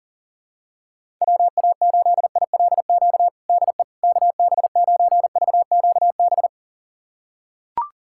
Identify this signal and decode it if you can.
Morse code, heard as a keyed tone, about 40 wpm, 700 Hz. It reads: WA8ILQ DE KB9VQB
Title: USA Callsign DE Callsign 40wpm